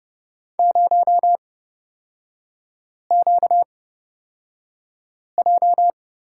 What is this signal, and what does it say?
Morse code, heard as a keyed tone, about 30 wpm, spaced as for 5 wpm, 700 Hz. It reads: 0QJ
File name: Lesson 38e - Letter q - Three Characters 30wpm